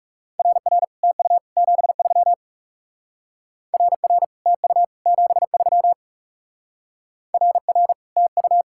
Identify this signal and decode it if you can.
Morse code, heard as a keyed tone, about 36 wpm, 700 Hz. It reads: RR TU 73 RR TU 73 RR TU
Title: Speed-Racing Common QSO Phrases Encore - 30wpm